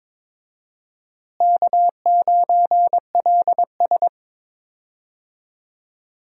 Morse code, heard as a keyed tone, about 22 wpm, 700 Hz. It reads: K9LS
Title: USA Call Signs - 4 Characters - Repeated 2x - 22wpm